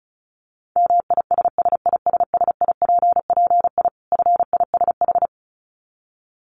Morse code, heard as a keyed tone, about 34 wpm, 700 Hz. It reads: MISSISSIPPI FISH
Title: Dit Challenge 35wpm